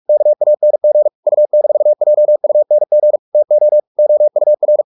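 Morse code, heard as a keyed tone, about 44 wpm, 600 Hz. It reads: KANG U-JUNG TO OUR
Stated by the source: Morse practice at 45 wpm, the novel Never by Ken Follett in English, from CW med Gnister.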